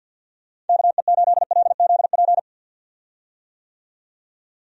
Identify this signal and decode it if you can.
Morse code, heard as a keyed tone, about 51 wpm, 700 Hz. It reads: KE8LZP
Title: USA Call Signs - 6 Characters - 50wpm